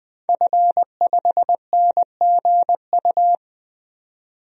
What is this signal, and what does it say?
Morse code, heard as a keyed tone, about 20 wpm, 700 Hz. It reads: F5NGU
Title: DXCC-100 Call Signs - 5 Characters - 20wpm